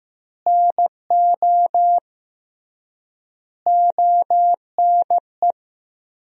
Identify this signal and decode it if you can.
Morse code, heard as a keyed tone, about 16 wpm, 700 Hz. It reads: NO ONE
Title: 3x Word-Spacing - Sentences from Top 400 Words - 15wpm